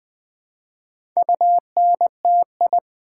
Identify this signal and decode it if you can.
Morse code, heard as a keyed tone, about 20 wpm, 700 Hz. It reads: UNTI
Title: Build A Word - 1 Letter At A Time - 3 to 5-Letter Words - 20wpm